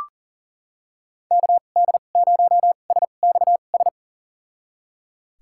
Morse code, heard as a keyed tone, about 39 wpm, spaced as for 28 wpm, 700 Hz. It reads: KD0SXS
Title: USA Call Signs - 6 Characters - 2x Character Spacing - 40wpm